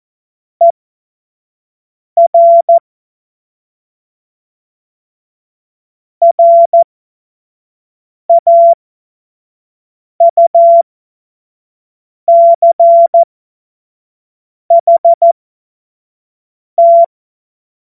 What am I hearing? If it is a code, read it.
Morse code, heard as a keyed tone, about 14 wpm, spaced as for 5 wpm, 680 Hz. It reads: ER RAUCHT
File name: Morsekurs / Witze / Witz 28. März 2023 25 F70